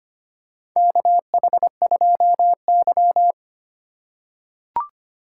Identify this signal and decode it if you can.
Morse code, heard as a keyed tone, about 25 wpm, 700 Hz. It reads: KH2Y
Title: USA Call Signs - 3x Character Spacing - 25wpm